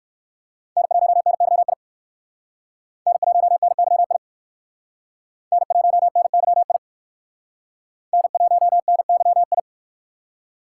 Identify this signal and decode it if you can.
Morse code, heard as a keyed tone, about 62 wpm, 700 Hz. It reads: N1NYI N1NYI N1NYI N1NYI